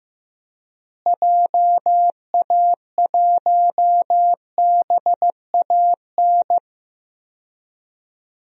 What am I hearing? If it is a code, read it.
Morse code, heard as a keyed tone, about 15 wpm, 700 Hz. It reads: JA1BAN
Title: DXCC-100 Call Signs - 6 Characters - Repeated 2x - 15wpm